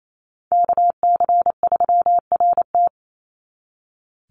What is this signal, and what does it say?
Morse code, heard as a keyed tone, about 28 wpm, 700 Hz. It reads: KC3RT
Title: NA SOTA Chaser Callsigns 28wpm